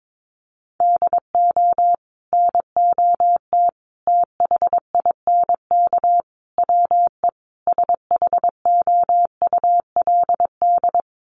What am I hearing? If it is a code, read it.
Morse code, heard as a keyed tone, about 22 wpm, 700 Hz. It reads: DO NOT THINK WE SHOULD
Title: Sets of 5 Words Encore 22wpm